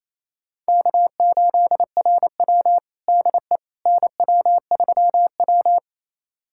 Morse code, heard as a keyed tone, about 28 wpm, 700 Hz. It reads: K8RW DE NW3W
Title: USA Callsign DE Callsign - 4 Characters - 28wpm